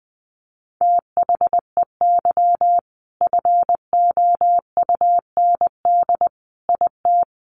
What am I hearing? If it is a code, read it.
Morse code, heard as a keyed tone, about 20 wpm, 700 Hz. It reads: THEY FOUND IT
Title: Sentences from Top 1400 Words 20wpm